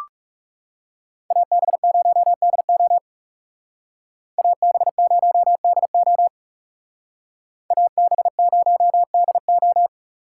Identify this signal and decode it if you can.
Morse code, heard as a keyed tone, about 36 wpm, 700 Hz. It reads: AB0DO AB0DO AB0DO
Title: Speed Racing - USA 2x2 Call Signs 30wpm